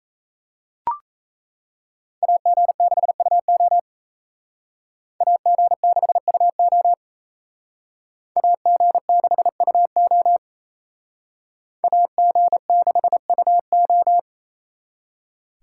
Morse code, heard as a keyed tone, about 35 wpm, 700 Hz. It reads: AG6UO AG6UO AG6UO AG6UO